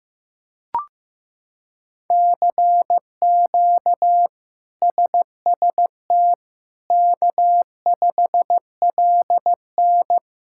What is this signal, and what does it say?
Morse code, heard as a keyed tone, about 15 wpm, 700 Hz. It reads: CQ SST K5LN
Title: K1USN SST QSO Practice - 15wpm